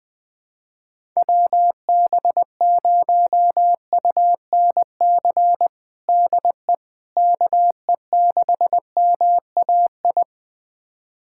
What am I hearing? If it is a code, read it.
Morse code, heard as a keyed tone, about 20 wpm, 700 Hz. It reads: WB0UNC DE KE6MAI
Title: USA Callsign DE Callsign 20wpm